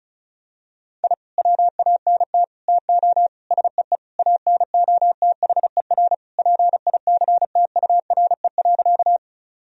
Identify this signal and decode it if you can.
Morse code, heard as a keyed tone, about 34 wpm, 700 Hz. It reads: I WANT TO SEE ANOTHER PICTURE.
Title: Sentences from Top 1000 Words - Review Part 1 - 35wpm